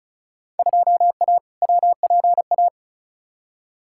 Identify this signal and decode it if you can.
Morse code, heard as a keyed tone, about 36 wpm, 700 Hz. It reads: 2A WPA
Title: Field Day Exchanges - 35wpm